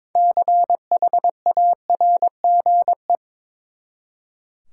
Morse code, heard as a keyed tone, about 22 wpm, 700 Hz. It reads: CHARGE
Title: Six Letter Words - 22wpm